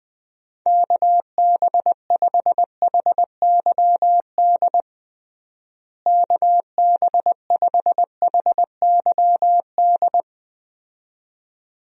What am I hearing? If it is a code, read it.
Morse code, heard as a keyed tone, about 21 wpm, 700 Hz. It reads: KB5HYD KB5HYD
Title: USA Call Signs - 6 Characters - Repeated 2x - 20wpm